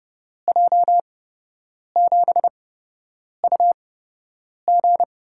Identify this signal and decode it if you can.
Morse code, heard as a keyed tone, about 30 wpm, spaced as for 8 wpm, 700 Hz. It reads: J7UG
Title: USA Call Signs - 5 Characters - 8x Character Spacing - 30wpm